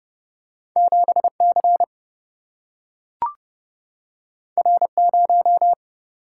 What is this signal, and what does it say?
Morse code, heard as a keyed tone, about 30 wpm, 700 Hz. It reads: 7C R0
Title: Single-Letter Single-Number 30wpm